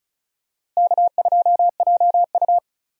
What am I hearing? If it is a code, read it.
Morse code, heard as a keyed tone, about 34 wpm, 700 Hz. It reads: K2JU
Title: USA Call Signs - 4 Characters - Repeated 2x - 35wpm